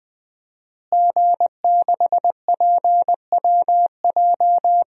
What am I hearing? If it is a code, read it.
Morse code, heard as a keyed tone, about 20 wpm, 700 Hz. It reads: G6PWJ
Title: DXCC-100 Call Signs - 5 Characters - Repeated 3x - 20wpm